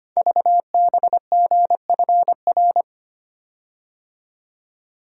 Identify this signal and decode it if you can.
Morse code, heard as a keyed tone, about 25 wpm, 700 Hz. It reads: VBGFR
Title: Code Groups - 5 Letters - 25wpm